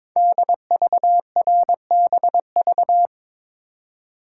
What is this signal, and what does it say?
Morse code, heard as a keyed tone, about 22 wpm, 700 Hz. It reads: DVRBV
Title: Code Groups - 5 Letters - 22wpm